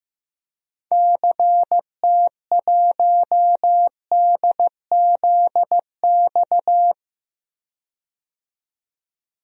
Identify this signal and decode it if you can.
Morse code, heard as a keyed tone, about 15 wpm, 700 Hz. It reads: CT1DZX
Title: DXCC-100 Call Signs - 6 Characters - 15wpm